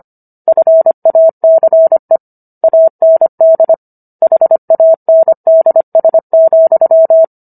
Morse code, heard as a keyed tone, about 25 wpm, 650 Hz. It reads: FACE AND HANDS,